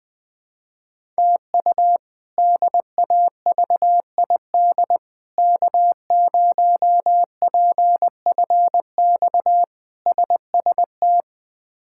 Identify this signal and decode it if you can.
Morse code, heard as a keyed tone, about 20 wpm, 700 Hz. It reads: TU DAVID K0PFX SST
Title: K1USN SST QSO Practice - 20wpm